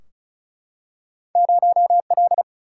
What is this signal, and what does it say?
Morse code, heard as a keyed tone, about 36 wpm, 700 Hz. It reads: 0 L
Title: Single-Letter Single-Number 35wpm